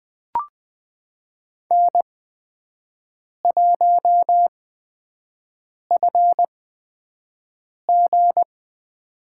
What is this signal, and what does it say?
Morse code, heard as a keyed tone, about 20 wpm, spaced as for 5 wpm, 700 Hz. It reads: N1FG
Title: USA Call Signs - 4 Characters - 8x Character Spacing - 20wpm